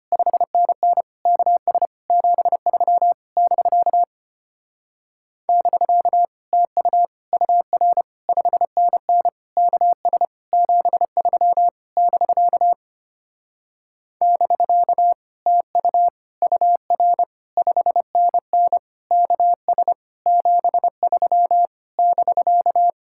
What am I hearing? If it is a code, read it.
Morse code, heard as a keyed tone, about 31 wpm, 700 Hz. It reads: 5NN KS 73 <BK> <BK> TU UR 5NN KS 73 <BK> <BK> TU UR 5NN KS 73 <BK>
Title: Speed-Racing Common QSO Phrases Encore - 25wpm